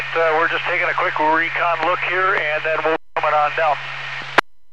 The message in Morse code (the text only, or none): none